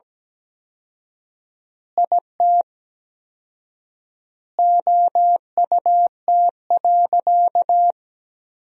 Morse code, heard as a keyed tone, about 17 wpm, 700 Hz. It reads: IT OUT.
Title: 4x Word-Spacing - Sentences from Top 300 Words - 17wpm